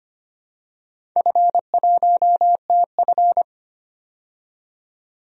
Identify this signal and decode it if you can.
Morse code, heard as a keyed tone, about 25 wpm, 700 Hz. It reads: F1TF
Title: DXCC-100 Call Signs - 4 Characters - 25wpm